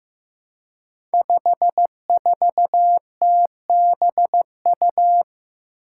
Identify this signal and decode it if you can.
Morse code, heard as a keyed tone, about 15 wpm, 700 Hz. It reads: 54TBU
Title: Five Challenging Letter-Number Combinations 15wpm